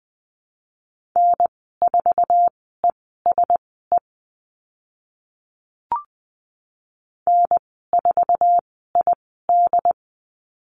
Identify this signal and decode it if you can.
Morse code, heard as a keyed tone, about 20 wpm, spaced as for 14 wpm, 700 Hz. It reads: N4ESE N4ID
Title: 069 Lesson 12g - Letter d - Call Signs 20wpm